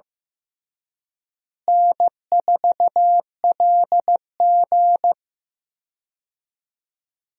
Morse code, heard as a keyed tone, about 15 wpm, 700 Hz. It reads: N4LG
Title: USA Call Signs - 4 Characters - Repeated 3x - 15wpm